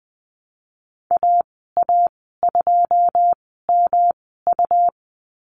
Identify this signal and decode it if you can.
Morse code, heard as a keyed tone, about 20 wpm, spaced as for 14 wpm, 700 Hz. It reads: AA2MU
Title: 161 Lesson 26g - Letter g - Call Signs 20wpm